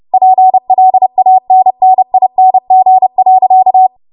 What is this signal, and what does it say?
Morse code, heard as a keyed tone, about 30 wpm, 750 Hz. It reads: PLANNING.
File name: Morse / 30WPM